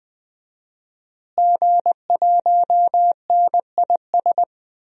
Morse code, heard as a keyed tone, about 20 wpm, 700 Hz. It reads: G1NIS
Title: DXCC-100 Call Signs - 5 Characters - Repeated 3x - 20wpm